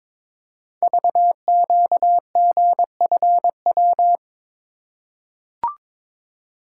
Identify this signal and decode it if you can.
Morse code, heard as a keyed tone, about 22 wpm, 690 Hz. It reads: VQGFW
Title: Code Groups - 5 Letters - 22wpm